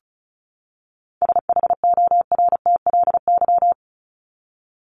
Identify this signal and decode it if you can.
Morse code, heard as a keyed tone, about 34 wpm, 700 Hz. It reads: SHORTLY
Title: LY Suffix - Top 100 Words with Base Words 35wpm